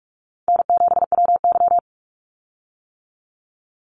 Morse code, heard as a keyed tone, about 46 wpm, 700 Hz. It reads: N7WY
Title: Easy Call Signs - 45wpm